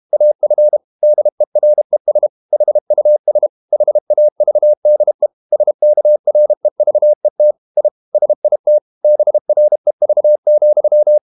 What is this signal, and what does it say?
Morse code, heard as a keyed tone, about 32 wpm, 600 Hz. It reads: AF DERES HUS HAVDE SKREVET I SIT BREV,
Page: Ferie 32 wpm